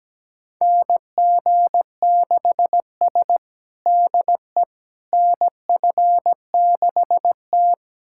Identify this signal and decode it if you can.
Morse code, heard as a keyed tone, about 17 wpm, 700 Hz. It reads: NG6S DE NF6T
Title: USA Callsign DE Callsign - 4 Characters - 17wpm